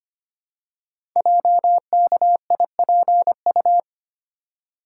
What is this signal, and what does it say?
Morse code, heard as a keyed tone, about 25 wpm, 700 Hz. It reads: JKIPU
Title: Code Groups - 5 Letters - 25wpm